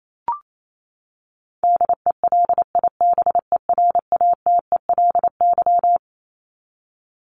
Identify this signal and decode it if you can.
Morse code, heard as a keyed tone, about 28 wpm, 700 Hz. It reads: DELIBERATELY
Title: Top 1751-2000 Words 28wpm